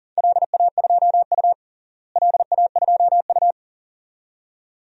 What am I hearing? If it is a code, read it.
Morse code, heard as a keyed tone, about 37 wpm, 690 Hz. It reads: LA2U LA2U
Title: DXCC-100 Call Signs - 4 Characters - Repeated 2x - 40wpm